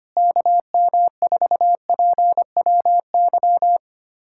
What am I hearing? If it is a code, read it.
Morse code, heard as a keyed tone, about 25 wpm, 700 Hz. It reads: KM4PWY
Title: USA Callsign DE Callsign 25wpm